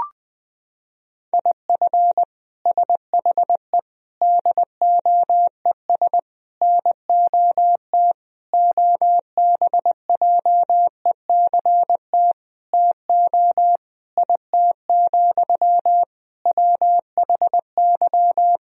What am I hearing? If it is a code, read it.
Morse code, heard as a keyed tone, about 21 wpm, 700 Hz. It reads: IF SHE DOES NOT OBJECT TO IT, WHY
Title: Sentences from Top 1000 Words - Review Part 3 - 20wpm